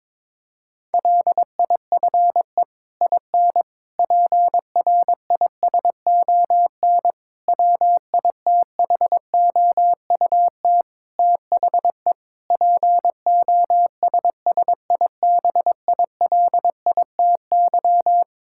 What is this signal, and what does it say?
Morse code, heard as a keyed tone, about 21 wpm, 700 Hz. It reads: LIFE IN PRISON WITHOUT THE POSSIBILITY
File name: practice / 22wpm / Sets of 8 Words - 22wpm